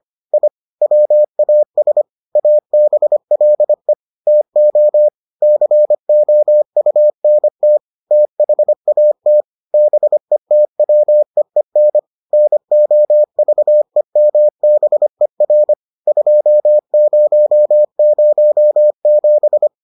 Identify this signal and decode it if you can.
Morse code, heard as a keyed tone, about 25 wpm, 600 Hz. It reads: I WAS ABLE TO COUNT THAT BETWEEN NOVEMBER 2007